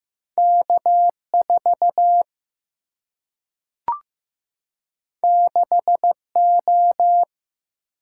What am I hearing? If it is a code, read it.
Morse code, heard as a keyed tone, about 15 wpm, 700 Hz. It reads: K 4 6 O
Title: Single-Letter Single-Number 15wpm